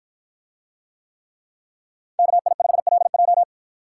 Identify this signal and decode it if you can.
Morse code, heard as a keyed tone, about 53 wpm, 700 Hz. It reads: KI5LJ